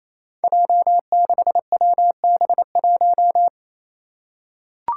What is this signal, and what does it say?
Morse code, heard as a keyed tone, about 28 wpm, 700 Hz. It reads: J6WB1
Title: Five Challenging Letter-Number Combinations 28wpm